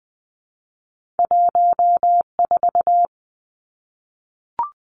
Morse code, heard as a keyed tone, about 20 wpm, 700 Hz. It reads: 14
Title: Two Challenging Letter-Number Combinations 20wpm